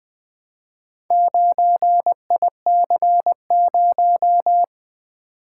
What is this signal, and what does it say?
Morse code, heard as a keyed tone, about 20 wpm, 700 Hz. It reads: 9IC0
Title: Four Letter-Number Combinations - 20wpm